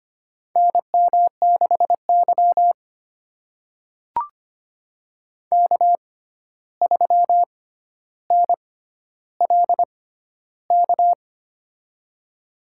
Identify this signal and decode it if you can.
Morse code, heard as a keyed tone, about 25 wpm, spaced as for 12 wpm, 700 Hz. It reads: NM6Y K3NLK
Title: USA Call Signs - 6x Character Spacing - 25wpm